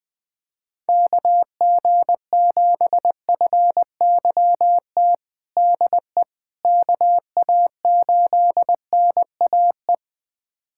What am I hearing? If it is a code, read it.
Morse code, heard as a keyed tone, about 20 wpm, 700 Hz. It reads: KG7FYT DE KA8NAE
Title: USA Callsign DE Callsign - 6 Characters - 20wpm